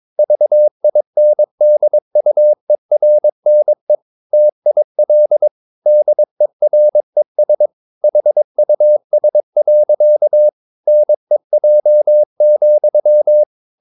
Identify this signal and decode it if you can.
Morse code, heard as a keyed tone, about 22 wpm, 600 Hz. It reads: VINDUERNE TIL DERES HUS. NEJ,
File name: Ferie QRS 22wpm